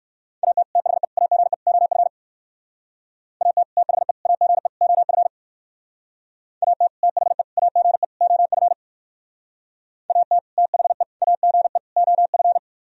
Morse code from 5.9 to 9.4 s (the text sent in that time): AT THE AGE OF